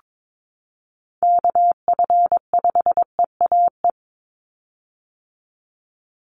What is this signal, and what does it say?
Morse code, heard as a keyed tone, about 22 wpm, 700 Hz. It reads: KF5EAE
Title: USA Call Signs 22wpm